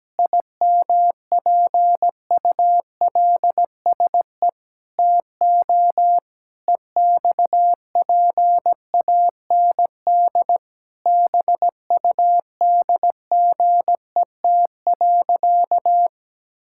Morse code, heard as a keyed tone, about 17 wpm, 700 Hz. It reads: IMPULSE TO EXPAND BUDGET.
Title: Pangram Challenge v2 - 17wpm